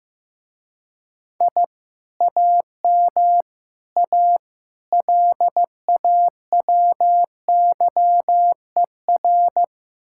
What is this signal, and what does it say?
Morse code, heard as a keyed tone, about 16 wpm, 700 Hz. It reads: I AM A LAWYER
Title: Speed-Racing Common QSO Phrases Encore - 15wpm